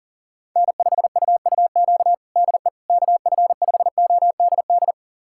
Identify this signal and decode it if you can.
Morse code, heard as a keyed tone, about 40 wpm, 700 Hz. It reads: N5UUQ DE KF5ODD
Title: USA Callsign DE Callsign 40wpm